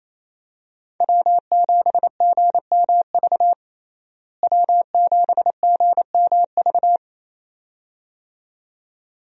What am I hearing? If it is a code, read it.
Morse code, heard as a keyed tone, about 28 wpm, 700 Hz. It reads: W7GMV W7GMV
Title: USA Call Signs - 5 Characters - Repeated 2x - 28wpm